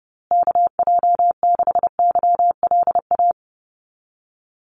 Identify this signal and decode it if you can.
Morse code, heard as a keyed tone, about 30 wpm, 700 Hz. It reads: KJ6YLA
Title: USA Call Signs Repeated 2x 30wpm